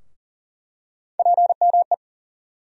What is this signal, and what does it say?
Morse code, heard as a keyed tone, about 40 wpm, 700 Hz. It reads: PME
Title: USA Call Sign Postfixes - 3 Letters 40wpm